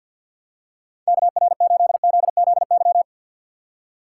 Morse code, heard as a keyed tone, about 49 wpm, 700 Hz. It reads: KR8ZZY
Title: USA Call Signs - 4x Character Spacing - 50wpm